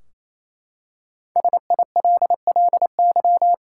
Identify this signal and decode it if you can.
Morse code, heard as a keyed tone, about 28 wpm, 700 Hz. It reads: SILLY
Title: Five Letter Words - 28wpm